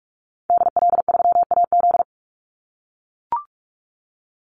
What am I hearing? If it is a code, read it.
Morse code, heard as a keyed tone, about 44 wpm, 700 Hz. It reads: DL3AZ
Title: Call Signs - 45wpm